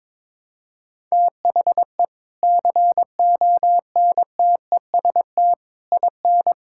Morse code, heard as a keyed tone, about 22 wpm, 700 Hz. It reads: THE CONTEST IN
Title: Sentences from Top 1600 Words 22wpm